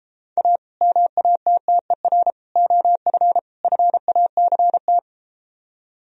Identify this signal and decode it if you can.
Morse code, heard as a keyed tone, about 33 wpm, 700 Hz. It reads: A MATTER OF FACT